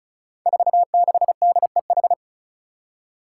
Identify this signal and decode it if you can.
Morse code, heard as a keyed tone, about 36 wpm, 700 Hz. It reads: 46DEH
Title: Five Challenging Letter-Number Combinations 35wpm